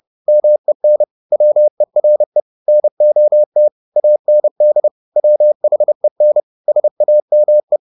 Morse code, heard as a keyed tone, about 30 wpm, 600 Hz. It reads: MEN WERE NOT AND WHEN SAME
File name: the old man and the sea / the old man and the sea 1 30wpm